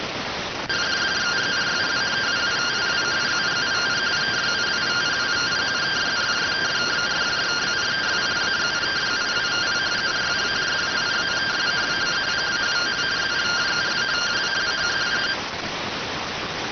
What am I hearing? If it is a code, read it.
Two-tone teleprinter signal, about 170 Hz shift.